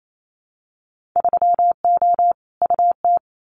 Morse code, heard as a keyed tone, about 28 wpm, 700 Hz. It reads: 3O UT
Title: Winter Field Day Exchanges 28wpm